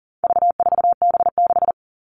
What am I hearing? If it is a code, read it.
Morse code, heard as a keyed tone, about 40 wpm, 700 Hz. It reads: V4B6
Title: All Cut Numbers - 4-Digit 40wpm